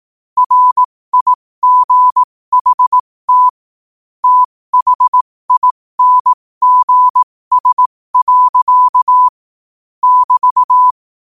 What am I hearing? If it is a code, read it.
Morse code, heard as a keyed tone, about 18 wpm, spaced as for 15 wpm, 1000 Hz. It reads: RIGHT THINGS. <BT>
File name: qotdfiles / 15wpm